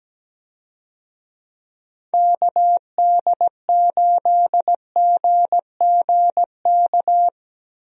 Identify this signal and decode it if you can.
Morse code, heard as a keyed tone, about 17 wpm, 700 Hz. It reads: KD8GGK
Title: USA Call Signs - 6 Characters - 17wpm